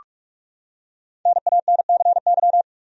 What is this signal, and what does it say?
Morse code, heard as a keyed tone, about 44 wpm, 700 Hz. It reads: NANCY
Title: USA Common Names - Max 5 Letters - 45wpm